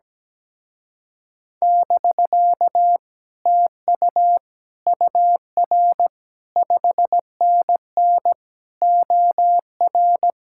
Morse code, heard as a keyed tone, about 17 wpm, 700 Hz. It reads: <BK> TU UR 5NN OR
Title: Speed-Racing Common QSO Phrases - 17wpm